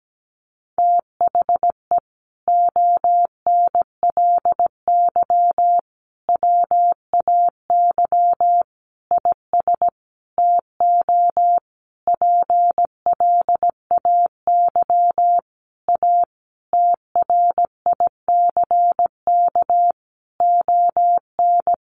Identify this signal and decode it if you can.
Morse code, heard as a keyed tone, about 17 wpm, 700 Hz. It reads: THE ONLY WAY IS TO PLAY A TRICK ON
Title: Sentences from Top 1500 Words 17wpm